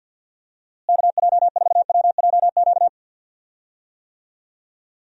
Morse code, heard as a keyed tone, about 51 wpm, 700 Hz. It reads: KJ4WJQ
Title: USA Call Signs - 6 Characters - Repeated 2x - 50wpm